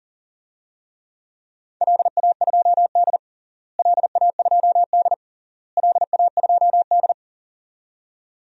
Morse code, heard as a keyed tone, about 40 wpm, 700 Hz. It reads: LA2D LA2D LA2D
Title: DXCC-100 Call Signs - 4 Characters - Repeated 3x - 40wpm